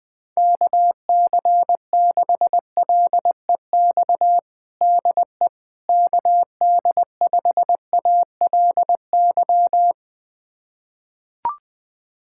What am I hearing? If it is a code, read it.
Morse code, heard as a keyed tone, about 21 wpm, 700 Hz. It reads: KC6LEX DE KD5ALY
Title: USA Callsign DE Callsign 20wpm